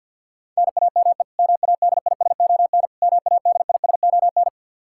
Morse code, heard as a keyed tone, about 53 wpm, 700 Hz. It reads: NAME MADISON MADISON
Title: K1USN SST QSO Practice 50wpm